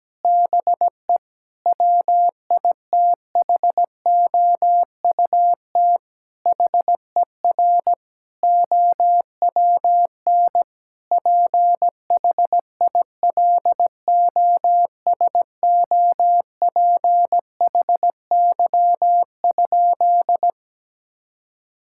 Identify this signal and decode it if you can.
Morse code, heard as a keyed tone, about 17 wpm, 700 Hz. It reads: BE WITHOUT HER OWN PHILOSOPHY?
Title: Sentences from Top 1000 Words - Review Part 6 - 17wpm